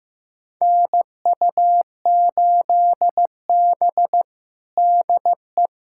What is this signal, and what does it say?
Morse code, heard as a keyed tone, about 16 wpm, 700 Hz. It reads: NU8B DE
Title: USA Callsign DE Callsign 15wpm